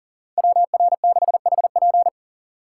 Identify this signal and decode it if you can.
Morse code, heard as a keyed tone, about 40 wpm, 700 Hz. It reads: WR6HP
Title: USA Call Signs - 2x Character Spacing - 40wpm